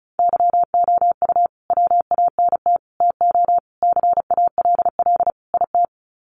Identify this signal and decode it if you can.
Morse code, heard as a keyed tone, about 34 wpm, 700 Hz. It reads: YOU WANT TO CALL IT
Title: Sets of 5 Words Encore2 35wpm